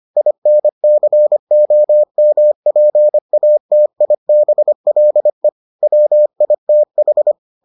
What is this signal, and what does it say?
Morse code, heard as a keyed tone, about 25 wpm, 600 Hz. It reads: INCOMPATIBLE WITH